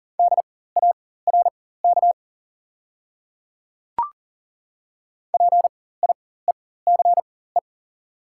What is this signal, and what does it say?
Morse code, heard as a keyed tone, about 40 wpm, spaced as for 19 wpm, 690 Hz. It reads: DARK PIECE
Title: 4x Character Spacing - Top 500 Words - 40wpm